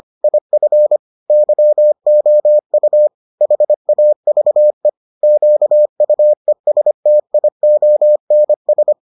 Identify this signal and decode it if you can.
Morse code, heard as a keyed tone, about 25 wpm, 600 Hz. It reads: IF YOU HAVE QUESTIONS